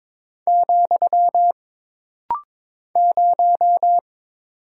Morse code, heard as a keyed tone, about 22 wpm, 700 Hz. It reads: , 0
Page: This rapid-fire timing is geared to push your ICR (Instant Character Recognition) skills.